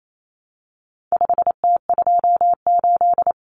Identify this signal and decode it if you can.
Morse code, heard as a keyed tone, about 28 wpm, 700 Hz. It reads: 5T28
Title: Common Cut Numbers - 4-Digit 28wpm